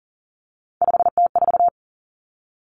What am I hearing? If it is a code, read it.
Morse code, heard as a keyed tone, about 40 wpm, 700 Hz. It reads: 5T4
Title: Common Cut Numbers - 3-Digit 40wpm